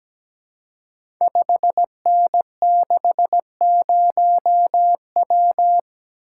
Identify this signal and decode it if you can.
Morse code, heard as a keyed tone, about 17 wpm, 700 Hz. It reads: 5N60W
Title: Code Groups - 5 Letter:Numbers - 17wpm